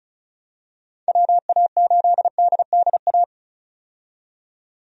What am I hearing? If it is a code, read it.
Morse code, heard as a keyed tone, about 36 wpm, 700 Hz. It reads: WA8DDA
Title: USA Call Signs - 6 Characters - 35wpm